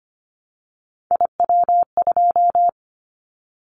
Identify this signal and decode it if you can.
Morse code, heard as a keyed tone, about 25 wpm, 700 Hz. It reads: IW2
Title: Callsign Prefixes for Top 25 DXCC - 25wpm